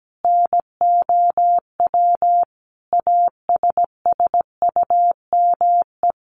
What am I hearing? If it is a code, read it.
Morse code, heard as a keyed tone, about 17 wpm, 700 Hz. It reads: NOW ASSUME
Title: Sentences from Top 1500 Words 17wpm